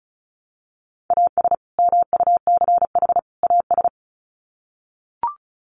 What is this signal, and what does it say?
Morse code, heard as a keyed tone, about 34 wpm, 700 Hz. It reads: AS MUCH AS
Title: Sets of 3 Words 35wpm